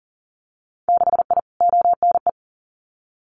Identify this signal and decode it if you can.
Morse code, heard as a keyed tone, about 38 wpm, 700 Hz. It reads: 6I ONE
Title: Winter Field Day Exchanges Encore 40wpm